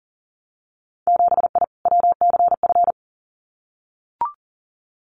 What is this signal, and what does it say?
Morse code, heard as a keyed tone, about 40 wpm, 700 Hz. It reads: 7I WCF
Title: Winter Field Day Exchanges Encore 40wpm